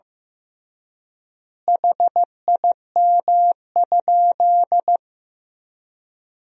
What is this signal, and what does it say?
Morse code, heard as a keyed tone, about 15 wpm, 700 Hz. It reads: HIM?
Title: 3x Word-Spacing - Sentences from Top 400 Words - 15wpm